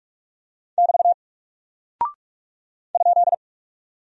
Morse code, heard as a keyed tone, about 45 wpm, 700 Hz. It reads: <BT> ?
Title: Single Character (Letter:Number:Punctuation:Prosign) Alt 45wpm